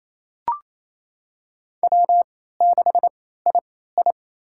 Morse code, heard as a keyed tone, about 28 wpm, spaced as for 16 wpm, 700 Hz. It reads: W6II
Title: USA Call Signs - 3x Character Spacing - 28wpm